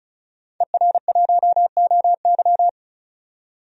Morse code, heard as a keyed tone, about 34 wpm, 700 Hz. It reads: ER1OY
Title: DXCC-100 Call Signs - 5 Characters - 35wpm